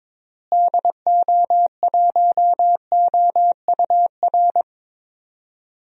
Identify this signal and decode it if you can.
Morse code, heard as a keyed tone, about 22 wpm, 700 Hz. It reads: DO1OUR
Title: DXCC-100 Call Signs - 6 Characters - Repeated 2x - 22wpm